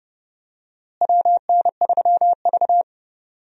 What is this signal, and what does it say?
Morse code, heard as a keyed tone, about 32 wpm, 700 Hz. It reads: WN3V
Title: USA Call Signs - 4 Characters - Repeated 3x - 30wpm